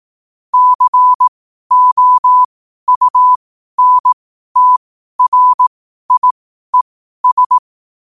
Morse code, heard as a keyed tone, about 18 wpm, spaced as for 12 wpm, 1000 Hz. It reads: COUNTRIES
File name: qotdfiles / 12wpm